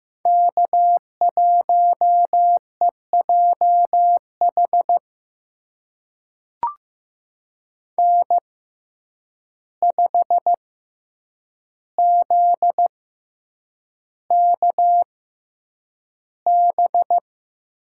Morse code, heard as a keyed tone, about 15 wpm, spaced as for 8 wpm, 700 Hz. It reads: K1EJH N5ZKB
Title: USA Call Signs - 5 Characters - 6x Character Spacing - 15wpm